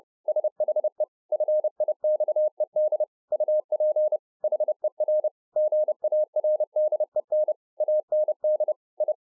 Morse code, heard as a keyed tone, about 30 wpm, 610 Hz. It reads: SHE FIXED UP HER GARDEN AND I